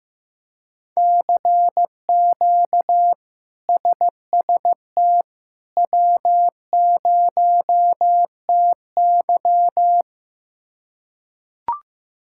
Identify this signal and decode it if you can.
Morse code, heard as a keyed tone, about 15 wpm, 700 Hz. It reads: CQ SST W0TY
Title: K1USN SST QSO Practice - 15wpm